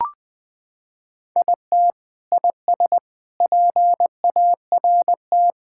Morse code, heard as a keyed tone, about 20 wpm, 700 Hz. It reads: IT IS PART
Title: Sentences with 4-Word Phrases - 20wpm